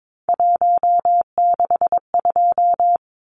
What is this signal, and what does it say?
Morse code, heard as a keyed tone, about 22 wpm, 700 Hz. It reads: 162
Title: Numbers - 3-Digit 22wpm